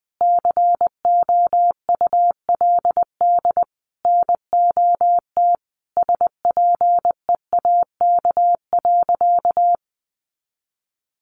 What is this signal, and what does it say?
Morse code, heard as a keyed tone, about 20 wpm, 700 Hz. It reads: COULD NOT SPEAK.
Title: Sentences from Top 1100 Words 20wpm